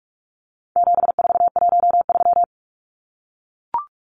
Morse code, heard as a keyed tone, about 44 wpm, 700 Hz. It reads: 7413
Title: Numbers - 4-Digit 45wpm